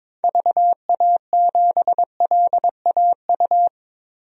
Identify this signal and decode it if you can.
Morse code, heard as a keyed tone, about 22 wpm, 700 Hz. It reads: VA7LAU
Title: DXCC-100 Call Signs - 6 Characters - 22wpm